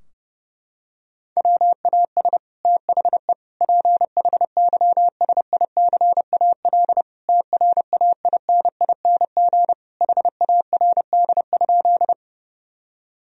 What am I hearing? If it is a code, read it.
Morse code, heard as a keyed tone, about 30 wpm, 700 Hz. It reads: WAS THE PHYSICAL TRAINING HARD?
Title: Sentences from Top 1000 Words - Review Part 4 - 30wpm